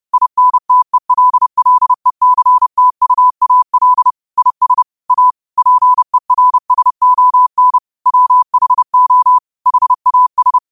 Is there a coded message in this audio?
Yes: Morse code from 0.1 to 10.6 s, heard as a keyed tone, about 30 wpm, 1000 Hz. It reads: INTELLECTUAL IS A PERSON WHO HAS